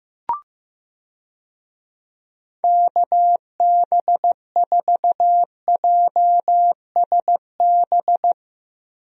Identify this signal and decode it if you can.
Morse code, heard as a keyed tone, about 15 wpm, 700 Hz. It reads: KB4JSB
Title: USA Call Signs - 6 Characters - 15wpm